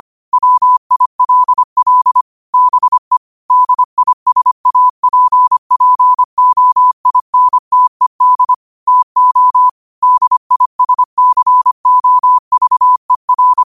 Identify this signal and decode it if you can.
Morse code, heard as a keyed tone, about 25 wpm, 1000 Hz. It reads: WILL BE DISAPPOINTED TO DISCOVER
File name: qotdfiles / 25wpm